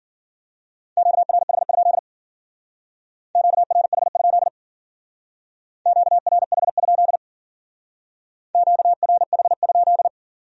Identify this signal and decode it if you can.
Morse code, heard as a keyed tone, about 48 wpm, 700 Hz. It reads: QRH? QRH? QRH? QRH?
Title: Speed-Racing Q-Codes - 40wpm